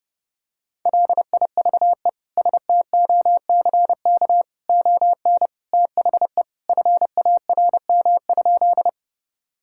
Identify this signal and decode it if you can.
Morse code, heard as a keyed tone, about 32 wpm, 700 Hz. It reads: LIVE STOCK ON THE FARM?
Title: Sentences from Top 1000 Words - Review Part 5 - 30wpm